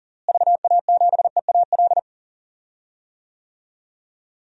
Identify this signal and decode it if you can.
Morse code, heard as a keyed tone, about 40 wpm, 700 Hz. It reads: VA7EAL
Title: DXCC-100 Call Signs - 6 Characters - Repeated 3x - 40wpm